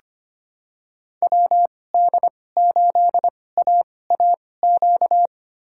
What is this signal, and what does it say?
Morse code, heard as a keyed tone, about 25 wpm, spaced as for 18 wpm, 700 Hz. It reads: WD8AAQ
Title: USA Call Signs - 2x Character Spacing - 25wpm